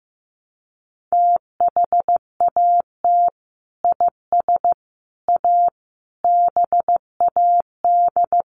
Morse code, heard as a keyed tone, about 15 wpm, 700 Hz. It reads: THAT IS A BAD
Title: Sentences from Top 700 Words 15wpm